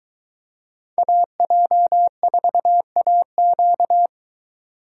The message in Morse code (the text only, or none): AJ4AQ